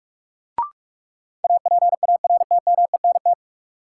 Morse code, heard as a keyed tone, about 43 wpm, 700 Hz. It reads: APARTMENT
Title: Sentences from Top 900 Words - New Word Drill - 45wpm